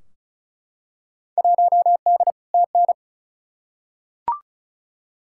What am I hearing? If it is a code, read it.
Morse code, heard as a keyed tone, about 36 wpm, 700 Hz. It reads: 1D TN
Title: Field Day Exchanges - 35wpm